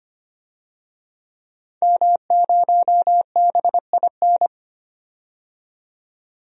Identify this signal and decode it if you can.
Morse code, heard as a keyed tone, about 25 wpm, 700 Hz. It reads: M0BIN
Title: DXCC-100 Call Signs - 5 Characters - 25wpm